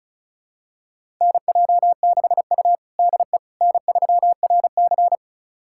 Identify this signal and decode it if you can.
Morse code, heard as a keyed tone, about 36 wpm, 700 Hz. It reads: NJ6U DE N3RC
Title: USA Callsign DE Callsign - 4 Characters - 35wpm